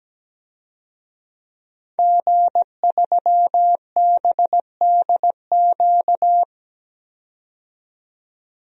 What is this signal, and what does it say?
Morse code, heard as a keyed tone, about 17 wpm, 700 Hz. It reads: G3BDQ
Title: DXCC-100 Call Signs - 5 Characters - 17wpm